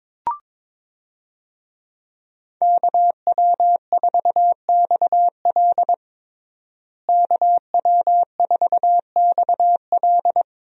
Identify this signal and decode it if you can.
Morse code, heard as a keyed tone, about 22 wpm, 700 Hz. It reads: KW4XL KW4XL
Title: USA Call Signs - 5 Characters - Repeated 3x - 22wpm